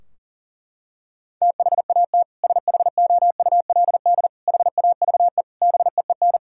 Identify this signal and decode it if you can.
Morse code, heard as a keyed tone, about 37 wpm, 700 Hz. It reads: THAT SHOULD HAVE BEEN
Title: Sentences from Top 1000 Words - Review Part 2 - 40wpm